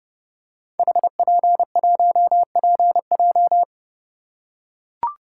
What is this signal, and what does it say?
Morse code, heard as a keyed tone, about 30 wpm, 700 Hz. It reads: HP1PJ
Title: Code Groups - 5 Letter:Numbers - 30wpm